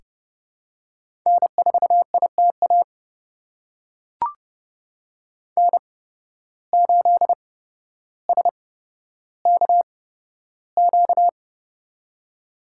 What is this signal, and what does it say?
Morse code, heard as a keyed tone, about 30 wpm, spaced as for 13 wpm, 700 Hz. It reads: N4ITA N8SKQ
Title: USA Call Signs - 5 Characters - 8x Character Spacing - 30wpm